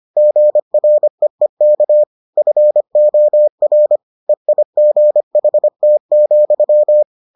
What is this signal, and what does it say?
Morse code, heard as a keyed tone, about 25 wpm, 600 Hz. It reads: GREEK FOR EIGHT,